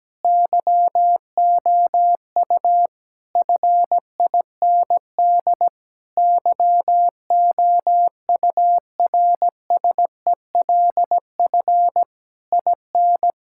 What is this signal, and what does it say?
Morse code, heard as a keyed tone, about 17 wpm, 700 Hz. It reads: YOU FIND YOURSELF IN
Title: Sentences from Top 1000 Words - Review Part 5 - 17wpm